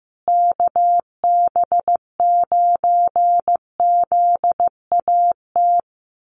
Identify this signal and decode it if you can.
Morse code, heard as a keyed tone, about 15 wpm, 700 Hz. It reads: KB9ZAT
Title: USA Call Signs Repeated 3x 15wpm